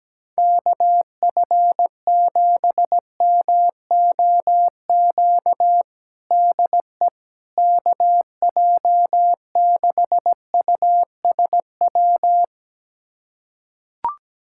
Morse code, heard as a keyed tone, about 17 wpm, 700 Hz. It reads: KF7MOQ DE KJ6USW
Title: USA Callsign DE Callsign - 6 Characters - 17wpm